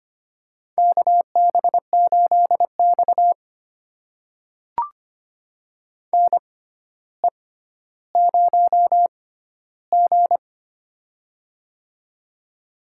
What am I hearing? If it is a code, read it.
Morse code, heard as a keyed tone, about 25 wpm, spaced as for 13 wpm, 700 Hz. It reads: KB8X NE0G
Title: USA Call Signs - 4 Characters - 6x Character Spacing - 25wpm